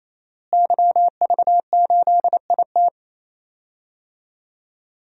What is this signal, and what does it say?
Morse code, heard as a keyed tone, about 28 wpm, 700 Hz. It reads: YV8IT
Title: DXCC-100 Call Signs - 5 Characters - Repeated 3x - 28wpm